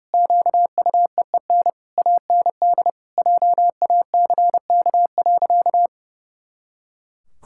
Morse code, heard as a keyed tone, about 32 wpm, 700 Hz. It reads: QUEEN AND JACK.
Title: Pangram Challenge v2 - 30wpm